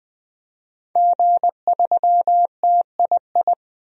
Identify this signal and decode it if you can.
Morse code, heard as a keyed tone, about 20 wpm, 700 Hz. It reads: G3TII
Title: DXCC-100 Call Signs - 5 Characters - 20wpm